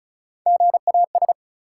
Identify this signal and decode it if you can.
Morse code, heard as a keyed tone, about 36 wpm, 700 Hz. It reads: GAS
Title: Three Letter Words - 35wpm